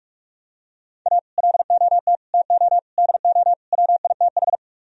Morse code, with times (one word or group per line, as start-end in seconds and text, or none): 1.1-1.2 A
1.4-2.2 LOT
2.3-2.8 TO
3.0-3.5 DO
3.7-4.6 WITH